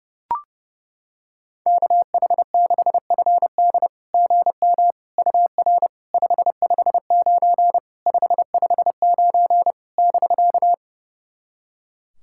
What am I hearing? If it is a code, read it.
Morse code, heard as a keyed tone, about 30 wpm, 690 Hz. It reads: KH6FD GM UR 559 559 <BK>
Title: SOTA QSO Practice 30wpm